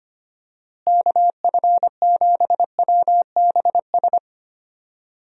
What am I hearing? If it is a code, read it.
Morse code, heard as a keyed tone, about 25 wpm, 700 Hz. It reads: KF7WBS
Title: USA Call Signs - 6 Characters - Repeated 3x - 25wpm